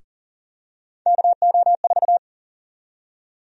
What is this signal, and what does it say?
Morse code, heard as a keyed tone, about 40 wpm, 700 Hz. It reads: KO4
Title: USA Call Sign Prefixes - Repeated 2x - 40wpm